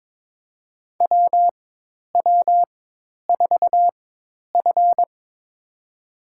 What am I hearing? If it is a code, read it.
Morse code, heard as a keyed tone, about 22 wpm, spaced as for 10 wpm, 700 Hz. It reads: WW4F
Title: USA Call Signs - 4 Characters - 4x Character Spacing - 22wpm